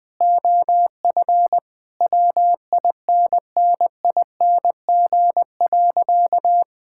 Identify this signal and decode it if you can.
Morse code, heard as a keyed tone, about 21 wpm, 700 Hz. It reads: OF WINNING.
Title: Sentences with 8-Word Phrases - 20wpm